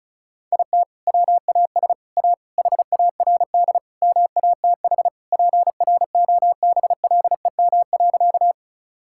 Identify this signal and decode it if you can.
Morse code, heard as a keyed tone, about 36 wpm, 690 Hz. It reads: IT WAS A HARD MATH PROBLEM.
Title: Sentences from Top 1000 Words - Review Part 6 - 35wpm